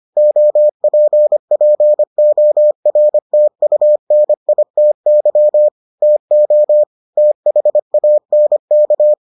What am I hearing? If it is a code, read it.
Morse code, heard as a keyed tone, about 25 wpm, 600 Hz. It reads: OPPORTUNITY TO THANK